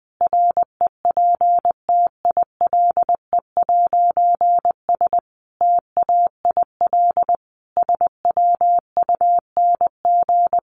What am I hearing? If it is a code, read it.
Morse code, heard as a keyed tone, about 20 wpm, 700 Hz. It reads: REPTILE'S TAIL SWUNG
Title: A Sound Of Thunder 20wpm